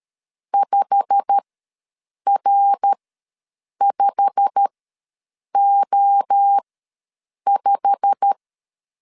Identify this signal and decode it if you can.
Morse code, heard as a keyed tone, about 13 wpm, spaced as for 7 wpm, 800 Hz. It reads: 5R5O5
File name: LEZIONI / 8 YJ45 / 8 YJ45 ws 12wpm 3146